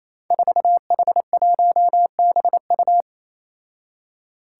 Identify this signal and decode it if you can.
Morse code, heard as a keyed tone, about 28 wpm, 700 Hz. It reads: 4H1BU
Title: Five Challenging Letter-Number Combinations 28wpm